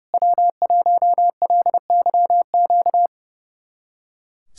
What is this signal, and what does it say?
Morse code, heard as a keyed tone, about 30 wpm, 700 Hz. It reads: W1LYQ
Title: SOTA QSO Practice 30wpm